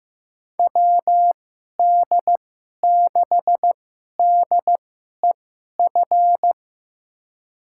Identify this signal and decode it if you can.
Morse code, heard as a keyed tone, about 15 wpm, spaced as for 11 wpm, 700 Hz. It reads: WD6DEF
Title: USA Call Signs - 6 Characters - 2x Character Spacing - 15wpm